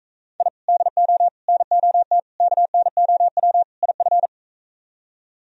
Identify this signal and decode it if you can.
Morse code, heard as a keyed tone, about 44 wpm, 700 Hz. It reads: I DO NOT KNOW IF